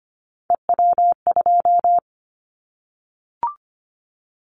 Morse code, heard as a keyed tone, about 25 wpm, 700 Hz. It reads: EW2
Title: Callsign Prefixes for Top 100 DXCC - 25wpm